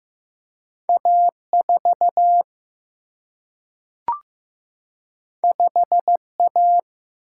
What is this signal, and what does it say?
Morse code, heard as a keyed tone, about 15 wpm, 700 Hz. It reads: A 4 5 A
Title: Single-Letter Single-Number 15wpm